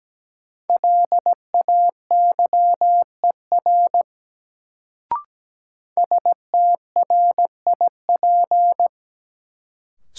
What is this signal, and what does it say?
Morse code, heard as a keyed tone, about 17 wpm, 700 Hz. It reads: LAYER STRIP
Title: Sentences from Top 1600 Words - New Word Drill Alt - 17wpm